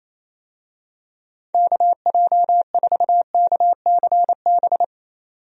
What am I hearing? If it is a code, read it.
Morse code, heard as a keyed tone, about 28 wpm, 700 Hz. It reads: KJ4KCB
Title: USA Call Signs - 6 Characters - 28wpm